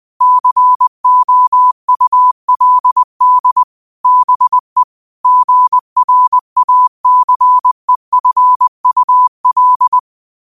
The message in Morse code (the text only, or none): COULD BE GRACEFUL